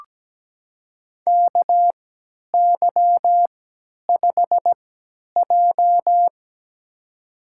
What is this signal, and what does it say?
Morse code, heard as a keyed tone, about 17 wpm, spaced as for 10 wpm, 700 Hz. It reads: KY5J
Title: USA Call Signs - 4 Characters - 3x Character Spacing - 17wpm